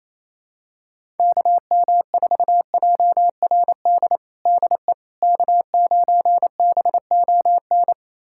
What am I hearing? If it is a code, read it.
Morse code, heard as a keyed tone, about 28 wpm, 700 Hz. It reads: KM4JRD DE K9BON
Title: USA Callsign DE Callsign 28wpm